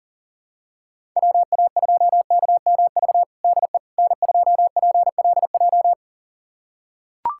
Morse code, heard as a keyed tone, about 37 wpm, 700 Hz. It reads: WA2KMV DE N2PLJ
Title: USA Callsign DE Callsign 40wpm